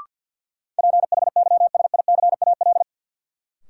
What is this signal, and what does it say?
Morse code, heard as a keyed tone, about 47 wpm, 700 Hz. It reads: PHYSICAL
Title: Sentences from Top 1000 Words - Review Part 4 - New Word Drill - 50wpm